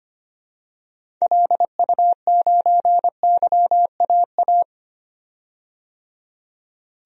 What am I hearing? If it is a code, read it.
Morse code, heard as a keyed tone, about 25 wpm, 700 Hz. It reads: LU9YAA
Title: DXCC-100 Call Signs - 6 Characters - 25wpm